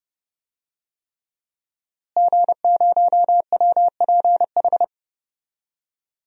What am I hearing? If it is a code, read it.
Morse code, heard as a keyed tone, about 32 wpm, 700 Hz. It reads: G0WPH
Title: DXCC-100 Call Signs - 5 Characters - Repeated 3x - 30wpm